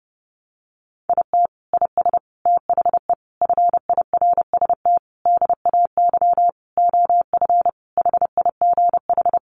Morse code, heard as a keyed tone, about 28 wpm, 700 Hz. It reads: IT IS THE FIRST DAY OF HIGH
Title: Additional Easy Phrases 30wpm